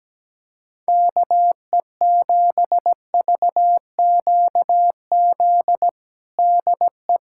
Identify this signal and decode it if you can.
Morse code, heard as a keyed tone, about 17 wpm, 700 Hz. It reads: KE7VQZ DE
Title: USA Callsign DE Callsign - 6 Characters - 17wpm